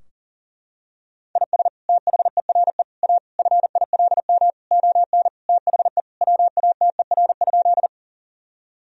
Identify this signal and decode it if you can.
Morse code, heard as a keyed tone, about 37 wpm, 700 Hz. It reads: IS THERE A FILM ON THE WATER?
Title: Sentences from Top 1000 Words - Review Part 3 - 40wpm